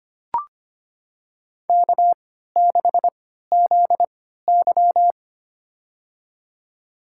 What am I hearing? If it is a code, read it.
Morse code, heard as a keyed tone, about 25 wpm, spaced as for 14 wpm, 700 Hz. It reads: K6ZY
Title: USA Call Signs - 4 Characters - 3x Character Spacing - 25wpm